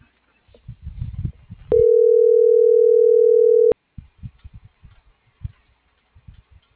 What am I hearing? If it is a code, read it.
Ringback tone (North America), 440 + 480 Hz.